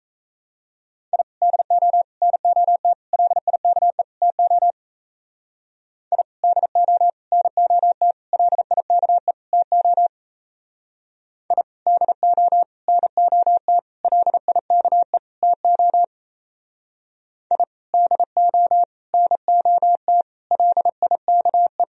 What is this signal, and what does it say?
Morse code, heard as a keyed tone, about 34 wpm, 700 Hz. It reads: I DO NOT LIKE TO I DO NOT LIKE TO I DO NOT LIKE TO I DO NOT LIKE
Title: Speed-Racing - Sets of 5 Words 28wpm